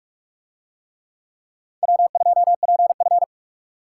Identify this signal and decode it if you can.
Morse code, heard as a keyed tone, about 44 wpm, 700 Hz. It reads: W2PF